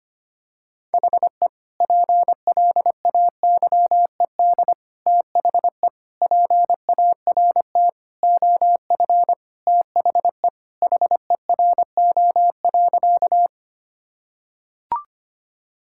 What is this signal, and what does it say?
Morse code, heard as a keyed tone, about 25 wpm, 700 Hz. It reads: HE PLAYED THE PART OF THE HERO.
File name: practice / 25wpm / Sentences with 4-Word Phrases - 25wpm